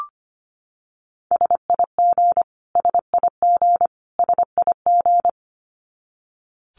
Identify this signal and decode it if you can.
Morse code, heard as a keyed tone, about 25 wpm, 700 Hz. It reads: SIG SIG SIG
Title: Abbreviations Repeated 3x - 25wpm